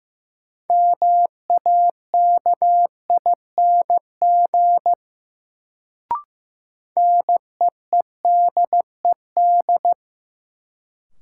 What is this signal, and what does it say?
Morse code, heard as a keyed tone, about 15 wpm, 700 Hz. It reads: MAKING NEEDED
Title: Sentences from Top 1000 Words - Review Part 6 - New Word Drill - 15wpm